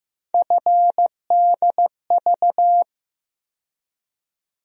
Morse code, heard as a keyed tone, about 15 wpm, 700 Hz. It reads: FDV
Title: USA Call Sign Postfixes - 3 Letters 15wpm